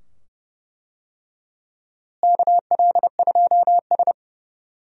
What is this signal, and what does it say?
Morse code, heard as a keyed tone, about 31 wpm, 700 Hz. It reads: KL2S
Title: USA Call Signs - 4 Characters - Repeated 2x - 30wpm